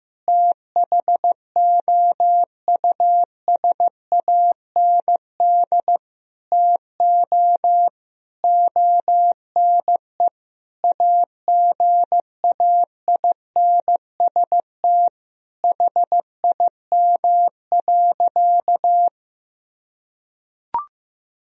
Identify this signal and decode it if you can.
Morse code, heard as a keyed tone, about 15 wpm, 700 Hz. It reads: THOUSAND TO ONE AGAINST HIM.
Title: Sentences from Top 1000 Words - Review Part 3 - 15wpm